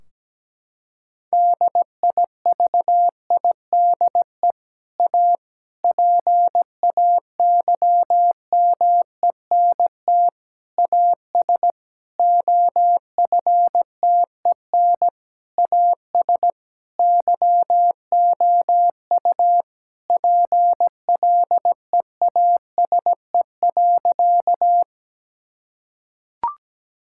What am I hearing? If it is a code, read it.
Morse code, heard as a keyed tone, about 17 wpm, 700 Hz. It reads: DIVIDE A PAYMENT AS OFTEN AS YOU PLEASE.
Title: Sentences from Top 1600 Words 17wpm